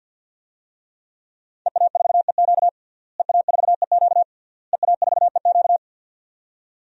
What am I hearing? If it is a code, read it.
Morse code, heard as a keyed tone, about 47 wpm, 700 Hz. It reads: EA4EQ EA4EQ EA4EQ
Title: DXCC-100 Call Signs - 5 Characters - Repeated 3x - 50wpm